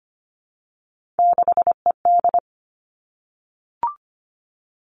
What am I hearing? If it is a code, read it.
Morse code, heard as a keyed tone, about 25 wpm, 700 Hz. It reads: 6ED
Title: Three Challenging Letter-Number Combinations 25wpm